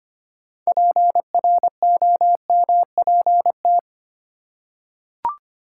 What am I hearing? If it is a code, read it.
Morse code, heard as a keyed tone, about 25 wpm, 700 Hz. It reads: PROMPT
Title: Six Letter Words - 25wpm